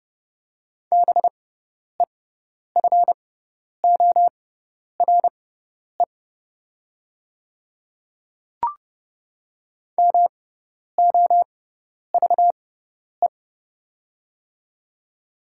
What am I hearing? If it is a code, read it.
Morse code, heard as a keyed tone, about 30 wpm, spaced as for 10 wpm, 700 Hz. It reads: BEFORE MOVE
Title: 6x Character Spacing - Top 200 Words - 30wpm